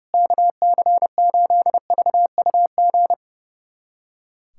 Morse code, heard as a keyed tone, about 30 wpm, 700 Hz. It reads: KC8VUG
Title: USA Callsign DE Callsign - 6 Characters - 30wpm